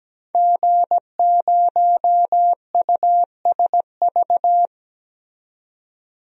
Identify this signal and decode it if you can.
Morse code, heard as a keyed tone, about 17 wpm, 700 Hz. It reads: G0USV
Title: DXCC-100 Call Signs - 5 Characters - 17wpm